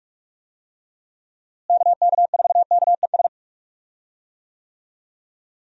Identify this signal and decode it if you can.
Morse code, heard as a keyed tone, about 44 wpm, 700 Hz. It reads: KK4KES
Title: USA Call Signs - 6 Characters - Repeated 2x - 45wpm